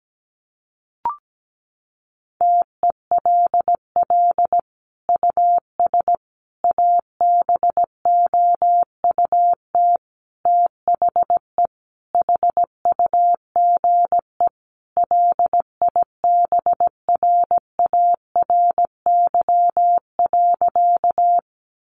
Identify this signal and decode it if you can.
Morse code, heard as a keyed tone, about 17 wpm, 700 Hz. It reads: TELL US ABOUT THE HUGE LIBRARY.
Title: Sentences from Top 1100 Words 17wpm